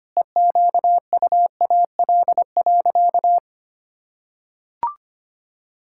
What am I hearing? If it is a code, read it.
Morse code, heard as a keyed tone, about 25 wpm, 700 Hz. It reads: EQUAL.
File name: practice / 25wpm / Sentences with 9-Word Phrases - 25wpm